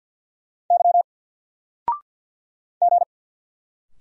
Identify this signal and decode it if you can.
Morse code, heard as a keyed tone, about 46 wpm, 700 Hz. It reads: Y G
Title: Single Letter-Number 50wpm